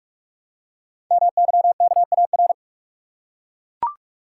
Morse code, heard as a keyed tone, about 44 wpm, 700 Hz. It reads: MYKAR
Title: Code Groups - 5 Letters - 45wpm